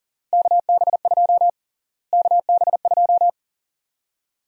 Morse code, heard as a keyed tone, about 40 wpm, 700 Hz. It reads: KB2 KB2
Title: USA Call Sign Prefixes - Repeated 2x - 40wpm